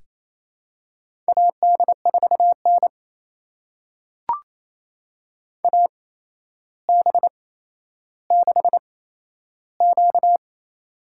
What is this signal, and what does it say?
Morse code, heard as a keyed tone, about 28 wpm, spaced as for 12 wpm, 700 Hz. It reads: AD4N AB6Q
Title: USA Call Signs - 4 Characters - 8x Character Spacing - 28wpm